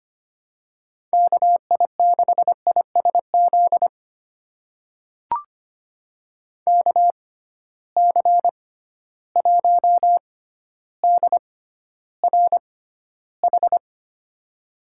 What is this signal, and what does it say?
Morse code, heard as a keyed tone, about 25 wpm, spaced as for 13 wpm, 700 Hz. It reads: KI6ISZ KC1DRH
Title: USA Call Signs - 6 Characters - 6x Character Spacing - 25wpm